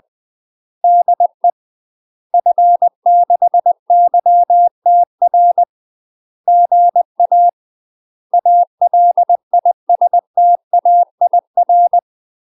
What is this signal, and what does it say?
Morse code, heard as a keyed tone, about 20 wpm, 700 Hz. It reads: DE F6YTR GA ALISTAIR
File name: UK MASTER QSOs TRAINING / 20WPM / UK MASTER QSOs 0065